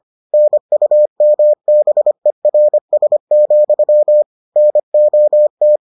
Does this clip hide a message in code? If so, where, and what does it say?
Yes: Morse code from 0.3 to 5.8 s, heard as a keyed tone, about 25 wpm, 600 Hz. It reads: NUMBERS, NOT